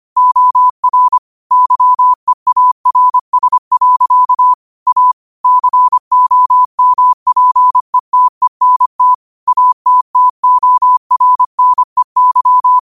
Morse code, heard as a keyed tone, about 25 wpm, 1000 Hz. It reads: OR YEARS. A COMPETENT ATTORNEY
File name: qotdfiles / 25wpm